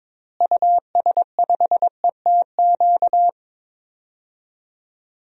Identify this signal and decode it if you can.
Morse code, heard as a keyed tone, about 22 wpm, 700 Hz. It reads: US5ETQ
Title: DXCC-100 Call Signs - 6 Characters - Repeated 3x - 22wpm